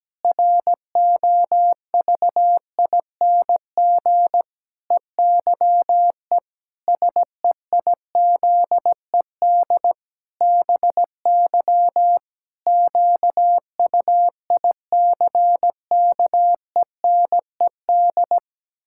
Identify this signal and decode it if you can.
Morse code, heard as a keyed tone, about 17 wpm, 700 Hz. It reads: ROVING EYE SEIZED BY QUICKENED
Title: Pangram Challenge v2 - 17wpm